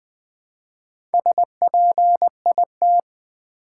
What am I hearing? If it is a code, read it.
Morse code, heard as a keyed tone, about 20 wpm, 700 Hz. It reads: SPIT
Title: Build A Word - 1 Letter At A Time - 5-Letter Words - 20wpm